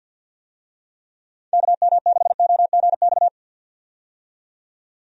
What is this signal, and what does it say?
Morse code, heard as a keyed tone, about 51 wpm, 700 Hz. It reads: KM6OGX
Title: USA Call Signs - 6 Characters - 50wpm